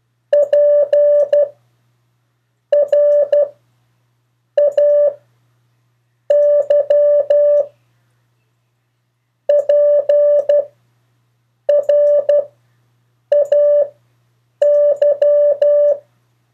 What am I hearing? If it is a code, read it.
Morse code, heard as a keyed tone, about 12 wpm, spaced as for 6 wpm, 580 Hz. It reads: PRAY PRAY
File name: 3of4-words-2